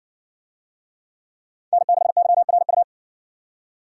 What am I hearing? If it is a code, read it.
Morse code, heard as a keyed tone, about 57 wpm, 700 Hz. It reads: N6YRU